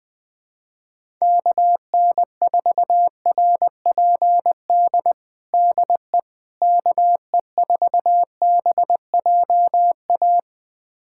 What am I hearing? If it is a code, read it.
Morse code, heard as a keyed tone, about 20 wpm, 700 Hz. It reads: KN4RPD DE KE4BJA
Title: USA Callsign DE Callsign - 6 Characters - 20wpm